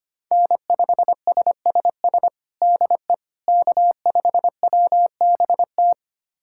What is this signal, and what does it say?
Morse code, heard as a keyed tone, about 25 wpm, 700 Hz. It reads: N5SSS DE K5WBT
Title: USA Callsign DE Callsign - 5 Characters - 25wpm